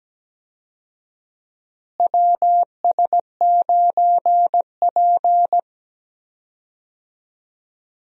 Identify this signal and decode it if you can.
Morse code, heard as a keyed tone, about 17 wpm, 700 Hz. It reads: WS9P
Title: USA Call Signs - 4 Characters - Repeated 2x - 17wpm